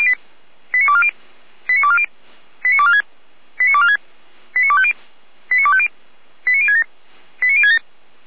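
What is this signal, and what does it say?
Five-tone selective call (ZVEI), heard as quick runs of five tones, tones of about 70 ms. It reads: 89240 89240 89246 89246 89250 89250 89067 89067
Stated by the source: Five-tone ZVEI selective call 89160 received by SDR, from feuerwehr-orlamuende fire brigade server.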